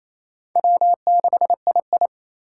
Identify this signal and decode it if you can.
Morse code, heard as a keyed tone, about 27 wpm, 700 Hz. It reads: W6II
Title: USA Call Signs - 4 Characters - 3x Character Spacing - 28wpm